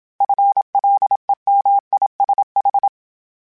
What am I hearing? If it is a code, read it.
Morse code, heard as a keyed tone, about 27 wpm, 800 Hz. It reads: FLEMISH